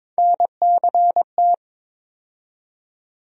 Morse code, heard as a keyed tone, about 22 wpm, 700 Hz. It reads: NCT
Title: USA Call Sign Postfixes - 3 Letters 22wpm